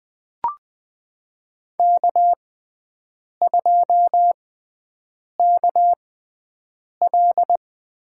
Morse code, heard as a keyed tone, about 20 wpm, spaced as for 7 wpm, 700 Hz. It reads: K2KL
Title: USA Call Signs - 5 Characters - 6x Character Spacing - 20wpm